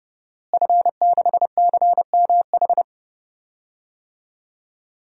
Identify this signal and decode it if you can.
Morse code, heard as a keyed tone, about 32 wpm, 700 Hz. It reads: F6CMH
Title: DXCC-100 Call Signs - 5 Characters - Repeated 3x - 30wpm